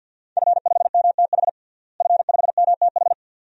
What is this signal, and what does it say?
Morse code, heard as a keyed tone, about 52 wpm, 700 Hz. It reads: F5MTH F5MTH
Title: DXCC-100 Call Signs - 5 Characters - Repeated 2x - 50wpm